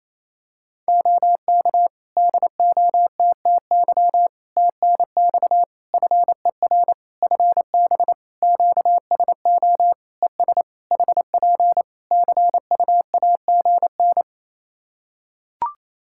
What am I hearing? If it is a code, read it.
Morse code, heard as a keyed tone, about 28 wpm, 700 Hz. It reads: OK DOTTY TNX FER FB QSO ES HP CUAGN
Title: Practice QSO - Exchange 4 - Ending - 28wpm